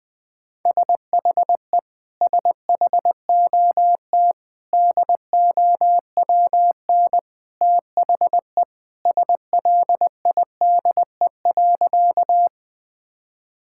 Morse code, heard as a keyed tone, about 20 wpm, 700 Hz. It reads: SHE SHOT DOWN THE SLIDE.
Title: Sentences from Top 1600 Words 20wpm